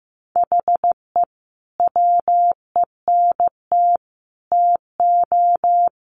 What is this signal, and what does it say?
Morse code, heard as a keyed tone, about 15 wpm, 700 Hz. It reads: HE WENT TO
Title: Sentences from Top 200 Words 15wpm